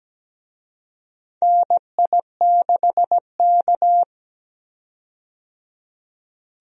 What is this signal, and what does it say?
Morse code, heard as a keyed tone, about 17 wpm, 700 Hz. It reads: NI6K
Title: USA Call Signs - 4 Characters - 17wpm